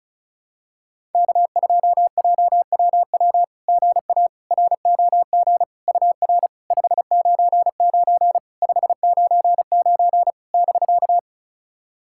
Morse code, heard as a keyed tone, about 36 wpm, 690 Hz. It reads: K2JWW GA ROG UR 599 599 <BK>
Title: SOTA QSO Practice 35wpm